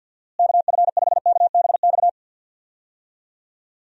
Morse code, heard as a keyed tone, about 52 wpm, 700 Hz. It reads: KF5KBX
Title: USA Call Signs - 6 Characters - Repeated 2x - 50wpm